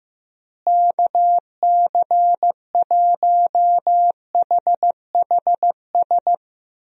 Morse code, heard as a keyed tone, about 15 wpm, 700 Hz. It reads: KC1HHS
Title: USA Call Signs - 6 Characters - Repeated 3x - 15wpm